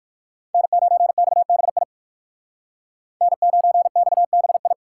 Morse code, heard as a keyed tone, about 58 wpm, 700 Hz. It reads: N9XBI N9XBI